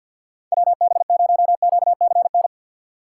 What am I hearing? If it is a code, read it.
Morse code, heard as a keyed tone, about 52 wpm, 700 Hz. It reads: WB0QCN
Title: USA Call Signs - 6 Characters - 3x Character Spacing - 50wpm